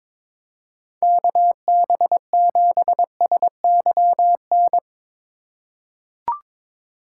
Morse code, heard as a keyed tone, about 23 wpm, 700 Hz. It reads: KB7SYN
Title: USA Call Signs - 6 Characters - 3x Character Spacing - 22wpm